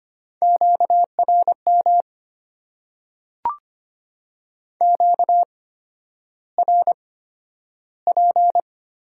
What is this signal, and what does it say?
Morse code, heard as a keyed tone, about 25 wpm, spaced as for 11 wpm, 700 Hz. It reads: QRM QRP
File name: Top 90 QSO Elements - 8x Character Spacing - 25wpm